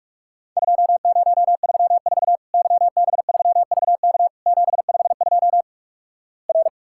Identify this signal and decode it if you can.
Morse code, heard as a keyed tone, about 46 wpm, 700 Hz. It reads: 2034 YB3VK 752 R
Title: Worked All Europe DX Contest - QTC Exchange - 45wpm